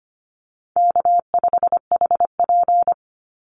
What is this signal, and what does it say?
Morse code, heard as a keyed tone, about 25 wpm, 700 Hz. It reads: K5HP
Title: Call Signs Repeated 3x - 25wpm